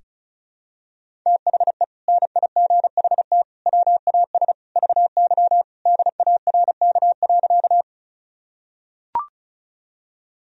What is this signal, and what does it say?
Morse code, heard as a keyed tone, about 34 wpm, 700 Hz. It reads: THE NIGHT WAS VY DARK.
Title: Sentences from Top 500 Words and Abbreviations 35wpm